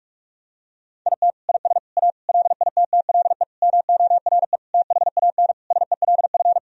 Morse code, heard as a keyed tone, about 46 wpm, 700 Hz. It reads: IT IS A LITTLE MORE THAN SELF
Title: Sentences from Top 1000 Words - Review Part 1 - 45wpm